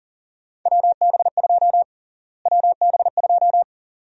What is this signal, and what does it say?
Morse code, heard as a keyed tone, about 40 wpm, 700 Hz. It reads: WB2 WB2
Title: USA Call Sign Prefixes - Repeated 2x - 40wpm